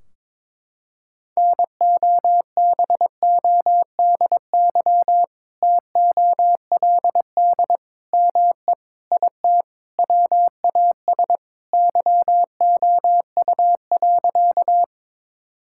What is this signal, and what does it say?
Morse code, heard as a keyed tone, about 22 wpm, 700 Hz. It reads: NOBODY TOLD ME IT WAS YOU.
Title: Sentences from Top 1600 Words 22wpm